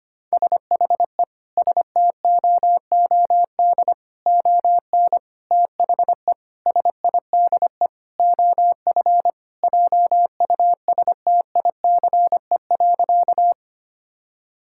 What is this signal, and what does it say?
Morse code, heard as a keyed tone, about 25 wpm, 700 Hz. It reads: SHE STOOD ON THE SIDE OF JUSTICE.
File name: practice / 25wpm / Sentences with 4-Word Phrases - 25wpm